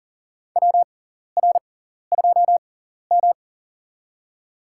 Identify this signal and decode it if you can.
Morse code, heard as a keyed tone, about 40 wpm, spaced as for 14 wpm, 700 Hz. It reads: WR2M
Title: USA Call Signs - 6x Character Spacing - 40wpm